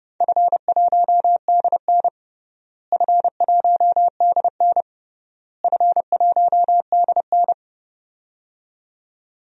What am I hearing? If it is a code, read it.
Morse code, heard as a keyed tone, about 30 wpm, 700 Hz. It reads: F1DN F1DN F1DN
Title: DXCC-100 Call Signs - 4 Characters - Repeated 3x - 30wpm